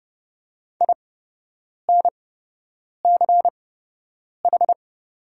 Morse code, heard as a keyed tone, about 31 wpm, spaced as for 8 wpm, 700 Hz. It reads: INCH
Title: 8x Character Spacing - Top 500 Words - 30wpm